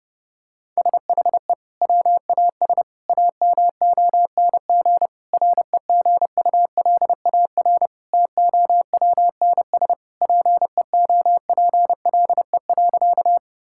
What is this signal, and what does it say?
Morse code, heard as a keyed tone, about 30 wpm, 700 Hz. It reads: SHE WAS AMONG REGULAR TOWNS PEOPLE.
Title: Sentences from Top 1000 Words - Review Part 6 - 30wpm